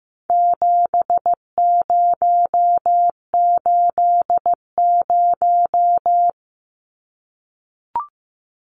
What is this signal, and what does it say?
Morse code, heard as a keyed tone, about 15 wpm, 700 Hz. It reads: 7080
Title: Numbers - 4-Digit 15wpm